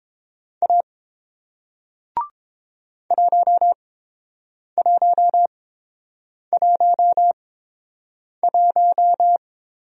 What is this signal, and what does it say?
Morse code, heard as a keyed tone, about 30 wpm, spaced as for 7 wpm, 690 Hz. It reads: A 1111